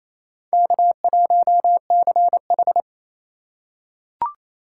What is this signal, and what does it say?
Morse code, heard as a keyed tone, about 28 wpm, 700 Hz. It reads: K1CH
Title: USA Call Signs - 6x Character Spacing - 28wpm